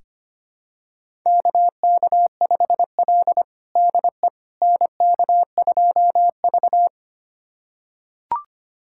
Morse code, heard as a keyed tone, about 25 wpm, 700 Hz. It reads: KK5L DE NK2V
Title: USA Callsign DE Callsign - 4 Characters - 25wpm